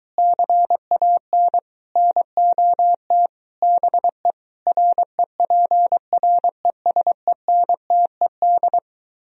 Morse code, heard as a keyed tone, about 23 wpm, 700 Hz. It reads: CAN NOT BE REPRESENTED